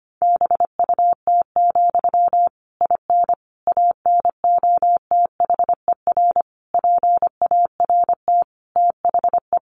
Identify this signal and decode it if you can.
Morse code, heard as a keyed tone, about 25 wpm, 700 Hz. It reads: BUT, IN ANOTHER PART THE
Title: Sentences from Top 1500 Words 25wpm